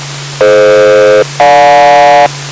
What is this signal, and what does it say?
Aircraft SELCAL, heard as two pairs of tones. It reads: DF-GJ